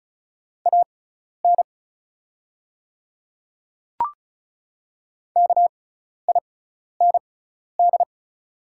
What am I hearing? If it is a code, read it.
Morse code, heard as a keyed tone, about 36 wpm, spaced as for 12 wpm, 700 Hz. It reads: AN KIND
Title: 6x Character Spacing - Top 200 Words - 35wpm